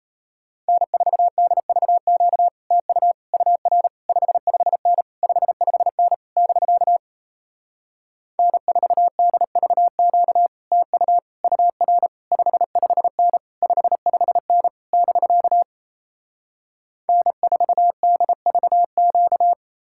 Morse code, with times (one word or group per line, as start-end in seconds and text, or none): 0.7-2.5 N4DVQ
2.7-3.1 TU
3.3-3.9 UR
4.1-5.0 55N
5.2-6.1 55N
6.4-7.0 <BK>
8.4-10.5 N4DVQ
10.7-11.2 TU
11.4-12.1 UR
12.3-13.4 55N
13.6-14.7 55N
14.9-15.6 <BK>
17.1-19.5 N4DVQ